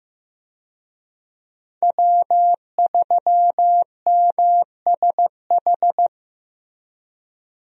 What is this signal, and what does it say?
Morse code, heard as a keyed tone, about 16 wpm, 700 Hz. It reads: W3MSH
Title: USA Call Signs - 5 Characters - Repeated 3x - 15wpm